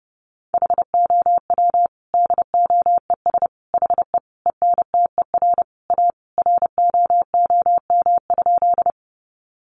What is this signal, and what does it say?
Morse code, heard as a keyed tone, about 30 wpm, 700 Hz. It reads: HOW DOES HE ENTER A ROOM?
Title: Sentences from Top 1300 Words 30wpm